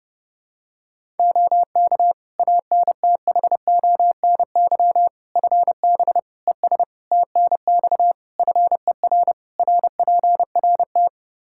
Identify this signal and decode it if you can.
Morse code, heard as a keyed tone, about 31 wpm, 700 Hz. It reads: OK ANTHONY FB ES TNX FER RPRT
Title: Practice QSO - Exchange 2 - Info - 30wpm